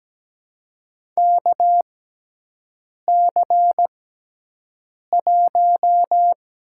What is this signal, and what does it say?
Morse code, heard as a keyed tone, about 17 wpm, spaced as for 6 wpm, 700 Hz. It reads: KC1
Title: USA Call Signs - 6 Characters - 6x Character Spacing - 17wpm